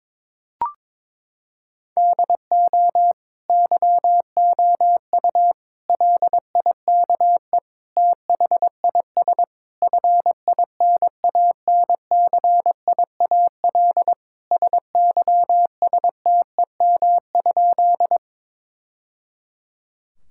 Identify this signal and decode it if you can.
Morse code, heard as a keyed tone, about 22 wpm, 700 Hz. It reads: DO YOU LIKE THIS FINANCIAL SYSTEM?
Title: Sentences from Top 1000 Words - Review Part 5 - 22wpm